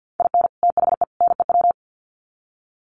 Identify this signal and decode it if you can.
Morse code, heard as a keyed tone, about 50 wpm, 700 Hz. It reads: IN THE NEW
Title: Sets of 3 Words 50wpm